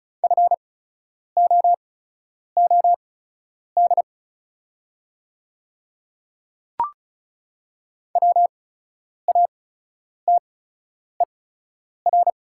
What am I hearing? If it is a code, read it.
Morse code, heard as a keyed tone, about 36 wpm, spaced as for 10 wpm, 690 Hz. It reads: FOOD WATER
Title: 8x Character Spacing - Top 500 Words - 35wpm